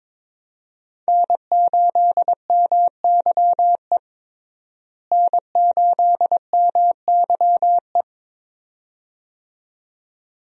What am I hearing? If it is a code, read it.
Morse code, heard as a keyed tone, about 22 wpm, 700 Hz. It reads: N8MYE N8MYE
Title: USA Call Signs - 5 Characters - Repeated 2x - 22wpm